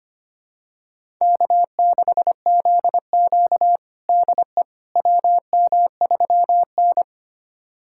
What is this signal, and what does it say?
Morse code, heard as a keyed tone, about 25 wpm, 700 Hz. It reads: K6ZQ DE WM3N
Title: USA Callsign DE Callsign - 4 Characters - 25wpm